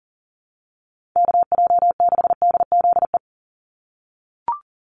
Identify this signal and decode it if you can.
Morse code, heard as a keyed tone, about 40 wpm, 700 Hz. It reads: KJ6DZE
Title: USA Call Signs 40wpm